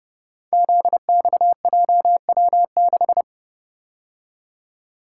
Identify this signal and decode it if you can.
Morse code, heard as a keyed tone, about 30 wpm, 700 Hz. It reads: ZXJW6
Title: Code Groups - 5 Letter:Numbers - 30wpm